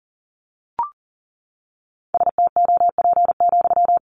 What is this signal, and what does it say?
Morse code, heard as a keyed tone, about 40 wpm, 700 Hz. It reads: STOP,
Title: Sentences from Top 300 Words 40wpm